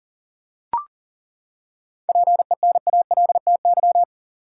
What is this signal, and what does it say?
Morse code, heard as a keyed tone, about 40 wpm, 700 Hz. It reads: PENALTY
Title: Seven Letter Words - 40wpm